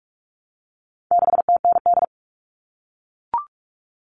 Morse code, heard as a keyed tone, about 44 wpm, 700 Hz. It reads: 6TND
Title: All Cut Numbers - 4-Digit 45wpm